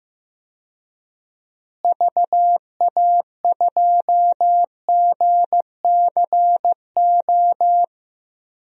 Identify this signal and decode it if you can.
Morse code, heard as a keyed tone, about 15 wpm, 700 Hz. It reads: VA2GCO
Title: DXCC-100 Call Signs - 6 Characters - Repeated 2x - 15wpm